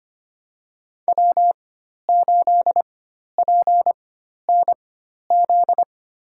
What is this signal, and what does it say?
Morse code, heard as a keyed tone, about 25 wpm, spaced as for 12 wpm, 700 Hz. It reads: W8PNZ
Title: USA Call Signs - 5 Characters - 4x Character Spacing - 25wpm